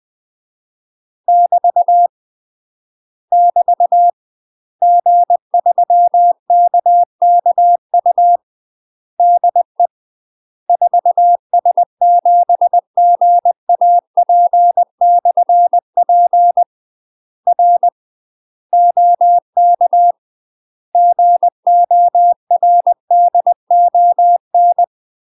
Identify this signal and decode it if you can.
Morse code, heard as a keyed tone, about 20 wpm, 700 Hz. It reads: <BT> <BT> G3KKU DE 4S7GAP/P R OK GORDON
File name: UK MASTER QSOs TRAINING / 20WPM / UK MASTER QSOs 0073